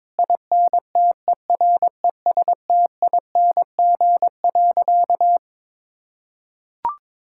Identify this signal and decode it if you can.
Morse code, heard as a keyed tone, about 22 wpm, 700 Hz. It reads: INTERESTING.
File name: practice / 22wpm / Sentences with 3-Word Phrases - 22wpm